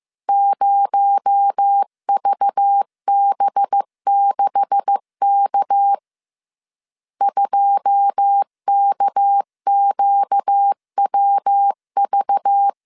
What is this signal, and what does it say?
Morse code, heard as a keyed tone, about 15 wpm, 800 Hz. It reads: 0VB6K 2KQWV
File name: CW / DET / 1 9 1500C S C 14wpm 3245